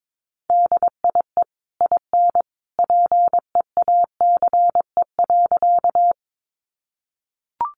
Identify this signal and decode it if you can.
Morse code, heard as a keyed tone, about 22 wpm, 700 Hz. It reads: DIE IN PEACE.
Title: Sentences from Top 1500 Words 22wpm